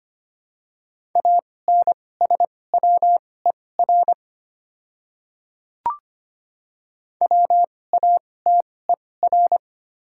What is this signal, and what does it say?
Morse code, heard as a keyed tone, about 25 wpm, spaced as for 18 wpm, 700 Hz. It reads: ANSWER WATER
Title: 2x Character Spacing - Top 300 Words - 25wpm